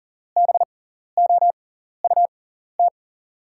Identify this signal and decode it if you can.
Morse code, heard as a keyed tone, about 40 wpm, spaced as for 14 wpm, 700 Hz. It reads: BOUT
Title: 6x Character Spacing - Top 300 Words - 40wpm